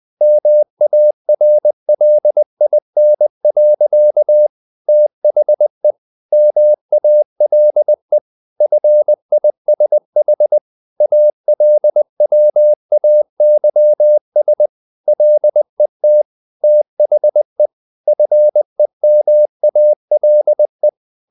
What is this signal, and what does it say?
Morse code, heard as a keyed tone, about 20 wpm, 600 Hz. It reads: MARLIN. THE MALE FISH ALWAYS LET THE FEMALE
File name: the old man and the sea / the old man and the sea part 1 20wpm